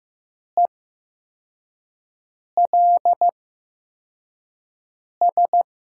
Morse code, heard as a keyed tone, about 15 wpm, spaced as for 4 wpm, 700 Hz. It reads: ELS
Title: USA Call Signs - 5 Characters - 8x Character Spacing - 15wpm